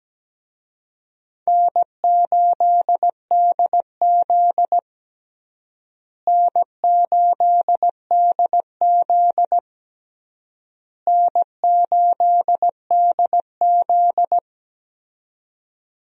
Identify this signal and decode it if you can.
Morse code, heard as a keyed tone, about 17 wpm, 700 Hz. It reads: N8DZ N8DZ N8DZ
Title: USA Call Signs - 4 Characters - Repeated 3x - 17wpm